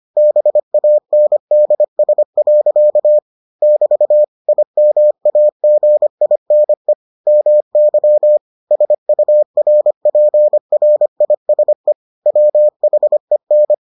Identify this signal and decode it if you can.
Morse code, heard as a keyed tone, about 25 wpm, 600 Hz. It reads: BANDS. <BT> IMAGINE MY SURPRISE WHEN